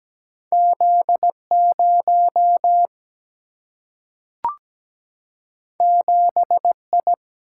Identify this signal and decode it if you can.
Morse code, heard as a keyed tone, about 17 wpm, 700 Hz. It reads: Z0 7I
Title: Single-Letter Single-Number 17wpm